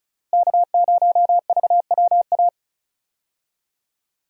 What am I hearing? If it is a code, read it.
Morse code, heard as a keyed tone, about 36 wpm, 700 Hz. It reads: K0VWA
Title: USA Call Signs - 5 Characters - Repeated 3x - 35wpm